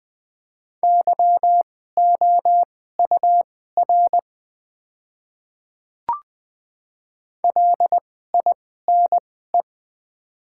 Morse code, heard as a keyed tone, about 20 wpm, spaced as for 14 wpm, 700 Hz. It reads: YOUR LINE
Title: 2x Character Spacing - Top 200 Words - 20wpm